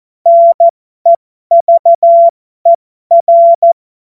Morse code, heard as a keyed tone, about 14 wpm, spaced as for 12 wpm, 680 Hz. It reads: NEVER
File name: Morsekurs / Witze